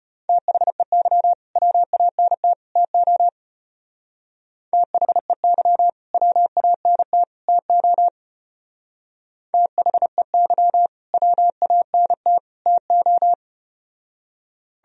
Morse code, heard as a keyed tone, about 32 wpm, 700 Hz. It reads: THEY WANT TO THEY WANT TO THEY WANT TO